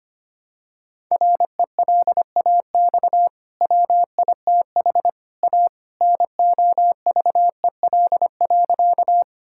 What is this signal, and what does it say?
Morse code, heard as a keyed tone, about 25 wpm, 700 Hz. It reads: RELAX WITH A NOVEL.
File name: Easy Sentences with Hints - 25wpm